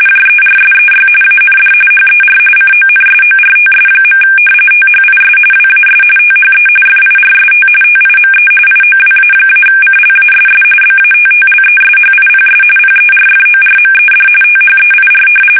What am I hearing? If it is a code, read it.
RTTY, about 45.45 baud. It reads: JERSEY AND HER CONSORTS. THIS WAS THE FIRST TIME IOWA-CLASS BATTLESHIPS PERFORMED THEIR
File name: fsk850 / battleshipnj